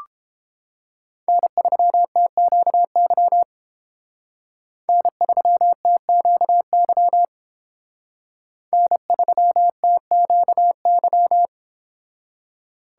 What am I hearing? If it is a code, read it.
Morse code, heard as a keyed tone, about 30 wpm, 700 Hz. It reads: N3TQY N3TQY N3TQY